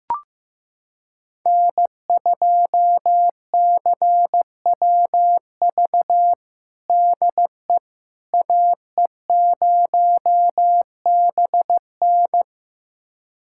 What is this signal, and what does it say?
Morse code, heard as a keyed tone, about 15 wpm, 700 Hz. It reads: N2CWV DE AE0BN
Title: USA Callsign DE Callsign - 5 Characters - 15wpm